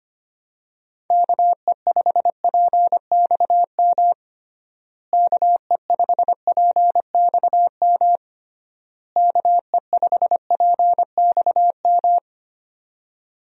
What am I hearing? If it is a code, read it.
Morse code, heard as a keyed tone, about 25 wpm, 700 Hz. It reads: KE5PXM KE5PXM KE5PXM
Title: USA Call Signs - 6 Characters - Repeated 3x - 25wpm